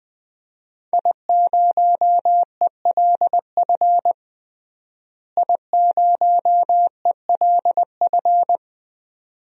Morse code, heard as a keyed tone, about 20 wpm, 700 Hz. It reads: I0ELF I0ELF
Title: DXCC-100 Call Signs - 5 Characters - Repeated 2x - 20wpm